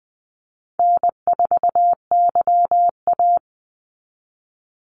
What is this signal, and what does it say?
Morse code, heard as a keyed tone, about 20 wpm, 700 Hz. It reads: N4YA
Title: Call Signs Repeated 3x - 20wpm